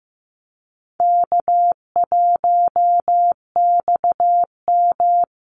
Morse code, heard as a keyed tone, about 15 wpm, 700 Hz. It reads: K1XM
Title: Call Signs Repeated 2x - 15wpm